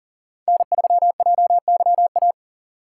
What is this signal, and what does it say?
Morse code, heard as a keyed tone, about 40 wpm, 700 Hz. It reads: N3JYA
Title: USA Call Signs - 5 Characters - 2x Character Spacing - 40wpm